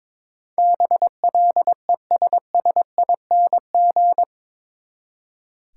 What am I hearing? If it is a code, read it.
Morse code, heard as a keyed tone, about 21 wpm, 700 Hz. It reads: BLESSING
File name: practice / 22wpm / Eight Letter Words - 22wpm